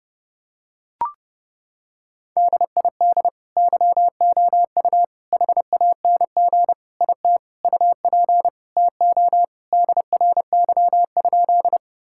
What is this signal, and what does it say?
Morse code, heard as a keyed tone, about 30 wpm, 700 Hz. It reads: DID YOU HANG IT UP TO DRY?
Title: Sentences from Top 1600 Words 30wpm